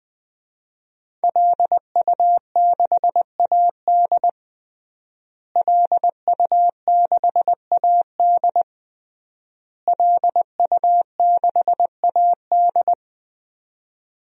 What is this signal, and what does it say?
Morse code, heard as a keyed tone, about 20 wpm, 700 Hz. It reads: LU6AD LU6AD LU6AD
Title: DXCC-100 Call Signs - 5 Characters - Repeated 3x - 20wpm